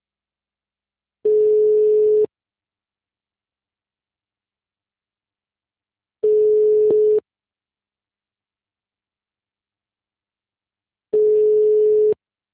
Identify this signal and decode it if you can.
Ringback tone (Europe, 425 Hz).